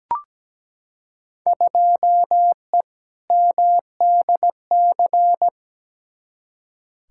Morse code, heard as a keyed tone, about 17 wpm, 700 Hz. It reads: 2E MDC
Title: Field Day Exchanges - 17wpm